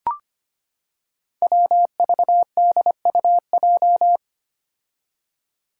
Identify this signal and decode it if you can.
Morse code, heard as a keyed tone, about 25 wpm, 700 Hz. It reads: WVDUJ
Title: Five Challenging Letter-Number Combinations 25wpm